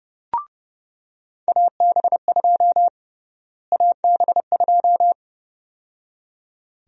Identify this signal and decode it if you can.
Morse code, heard as a keyed tone, about 31 wpm, 700 Hz. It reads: AB2 AB2
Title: USA Call Sign Prefixes - Repeated 2x - 30wpm